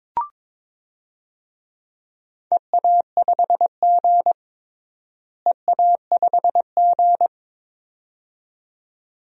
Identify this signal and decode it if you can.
Morse code, heard as a keyed tone, about 22 wpm, 700 Hz. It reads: EA5G EA5G
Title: DXCC-100 Call Signs - 4 Characters - Repeated 2x - 22wpm